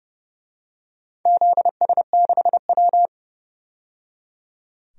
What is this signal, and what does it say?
Morse code, heard as a keyed tone, about 32 wpm, 700 Hz. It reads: ZS6W
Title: DXCC-100 Call Signs - 4 Characters - 30wpm